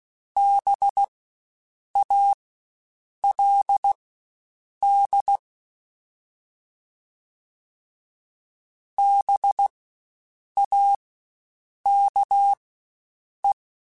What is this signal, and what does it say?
Morse code, heard as a keyed tone, about 16 wpm, spaced as for 7 wpm, 800 Hz. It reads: BALD BAKE
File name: Spaced Alphas / Spaced Alphas mp3s / Bs sp @15